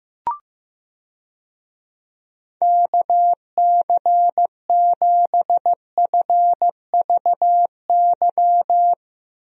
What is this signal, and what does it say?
Morse code, heard as a keyed tone, about 15 wpm, 700 Hz. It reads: KC7FVY
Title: USA Call Signs - 6 Characters - Repeated 2x - 15wpm